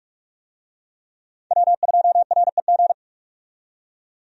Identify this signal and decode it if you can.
Morse code, heard as a keyed tone, about 48 wpm, 700 Hz. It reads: W2REG
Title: USA Call Signs - 5 Characters - 45wpm